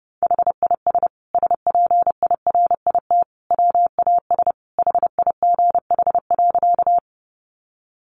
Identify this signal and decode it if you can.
Morse code, heard as a keyed tone, about 30 wpm, 700 Hz. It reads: HIS SPIRIT WAS HIGH.
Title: Sentences from Top 1400 Words 30wpm